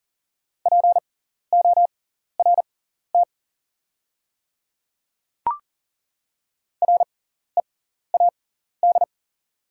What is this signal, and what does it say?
Morse code, heard as a keyed tone, about 38 wpm, spaced as for 14 wpm, 700 Hz. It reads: PORT READ
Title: 6x Character Spacing - Top 500 Words - 40wpm